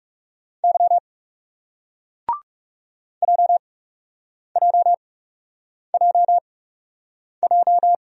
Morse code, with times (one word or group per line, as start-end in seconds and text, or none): 0.6-1.0 Y
3.2-7.9 JJJJ